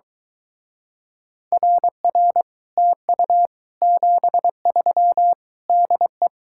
Morse code, heard as a keyed tone, about 23 wpm, 700 Hz. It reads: RR TU 73 DE